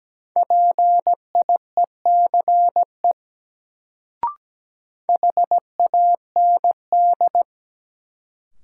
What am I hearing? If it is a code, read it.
Morse code, heard as a keyed tone, about 17 wpm, 700 Hz. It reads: PIECE HAND
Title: Sets of 3 Words - New Word Drill - 17wpm